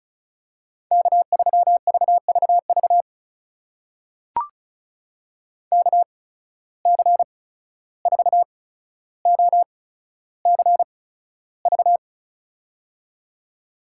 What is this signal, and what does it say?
Morse code, heard as a keyed tone, about 36 wpm, spaced as for 14 wpm, 700 Hz. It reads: K3VVV KC4OCV
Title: USA Call Signs - 8x Character Spacing - 35wpm